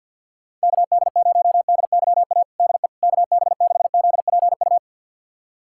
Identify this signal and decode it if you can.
Morse code, heard as a keyed tone, about 51 wpm, 700 Hz. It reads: KD0DYA DE KB6ZPU
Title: USA Callsign DE Callsign 50wpm